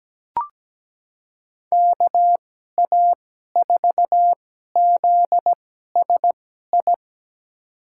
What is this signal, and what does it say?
Morse code, heard as a keyed tone, about 17 wpm, spaced as for 12 wpm, 700 Hz. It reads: KA4ZSI
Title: USA Call Signs - 2x Character Spacing - 17wpm